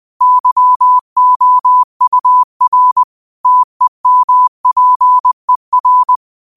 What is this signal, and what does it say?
Morse code, heard as a keyed tone, about 20 wpm, 1000 Hz. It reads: YOUR TEMPER